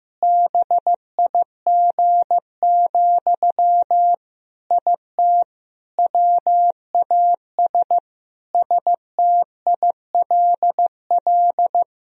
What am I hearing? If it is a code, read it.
Morse code, heard as a keyed tone, about 15 wpm, 700 Hz. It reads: BIG, IT WAS STILL
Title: Sentences from Top 1000 Words - Review Part 3 - 15wpm